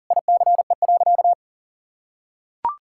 Morse code, heard as a keyed tone, about 40 wpm, 700 Hz. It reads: ICE.
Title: Sentences from Top 1600 Words 40wpm